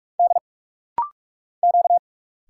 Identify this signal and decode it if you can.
Morse code, heard as a keyed tone, about 46 wpm, 700 Hz. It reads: D Q
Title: Single Letters - Rapid-Fire 45wpm